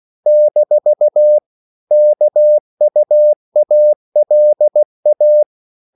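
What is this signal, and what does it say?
Morse code, heard as a keyed tone, about 16 wpm, 600 Hz. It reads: - KUALA
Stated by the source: CW practice of national capitals at 16 words per minute, from ON6ZQ.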